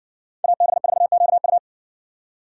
Morse code, heard as a keyed tone, about 57 wpm, 700 Hz. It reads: AB4QU